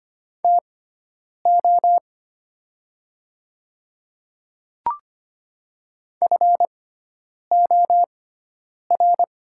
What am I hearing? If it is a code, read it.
Morse code, heard as a keyed tone, about 25 wpm, spaced as for 9 wpm, 700 Hz. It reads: TO FOR
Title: 6x Character Spacing - Top 200 Words - 25wpm